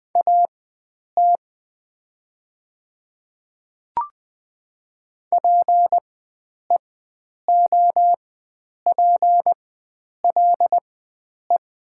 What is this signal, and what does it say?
Morse code, heard as a keyed tone, about 20 wpm, spaced as for 9 wpm, 700 Hz. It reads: AT PEOPLE
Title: 4x Character Spacing - Top 100 Words - 20wpm